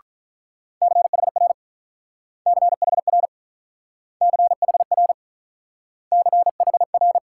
Morse code, heard as a keyed tone, about 46 wpm, 700 Hz. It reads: CHR CHR CHR CHR